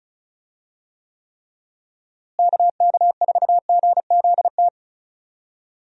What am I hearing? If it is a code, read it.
Morse code, heard as a keyed tone, about 34 wpm, 700 Hz. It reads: KK4GZT
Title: USA Call Signs - 6 Characters - Repeated 3x - 35wpm